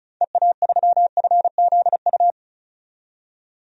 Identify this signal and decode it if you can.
Morse code, heard as a keyed tone, about 34 wpm, 700 Hz. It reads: EA3FZU
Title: DXCC-100 Call Signs - 6 Characters - 35wpm